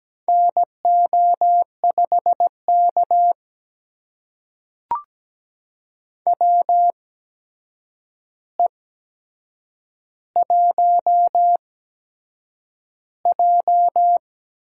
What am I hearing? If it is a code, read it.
Morse code, heard as a keyed tone, about 17 wpm, spaced as for 7 wpm, 700 Hz. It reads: NO5K WE1J
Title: USA Call Signs - 4 Characters - 8x Character Spacing - 17wpm